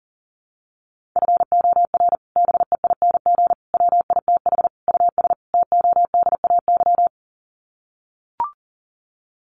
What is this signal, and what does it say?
Morse code, heard as a keyed tone, about 40 wpm, 700 Hz. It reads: FOR BEING WITH US TODAY
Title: Sets of 5 Words Encore2 40wpm